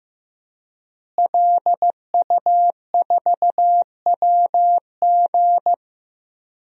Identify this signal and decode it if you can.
Morse code, heard as a keyed tone, about 15 wpm, 700 Hz. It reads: LU4WG
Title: DXCC-100 Call Signs - 5 Characters - Repeated 3x - 15wpm